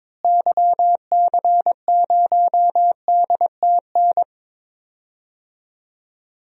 Morse code, heard as a keyed tone, about 22 wpm, 700 Hz. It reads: YC0DTN
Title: DXCC-100 Call Signs - 6 Characters - Repeated 2x - 22wpm